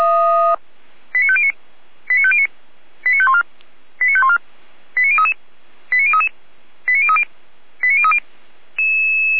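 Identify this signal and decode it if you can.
Five-tone selective call (ZVEI), heard as quick runs of five tones, tones of about 80 ms. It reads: 89509 89509 89514 89514 89033 89033 89030 89030
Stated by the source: Five-tone ZVEI selective call 89001, received by SDR, from the Feuerwehr Orlamünde audio server.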